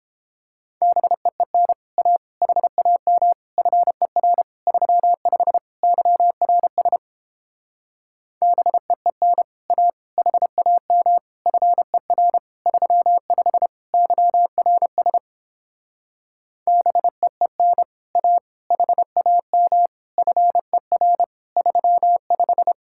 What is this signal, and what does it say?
Morse code, heard as a keyed tone, about 29 wpm, 700 Hz. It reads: BEEN A HAM FER 35 YRS BEEN A HAM FER 35 YRS BEEN A HAM FER 35